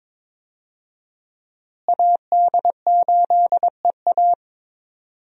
Morse code, heard as a keyed tone, about 22 wpm, 700 Hz. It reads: AD8EA
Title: USA Call Signs - 5 Characters - Repeated 3x - 22wpm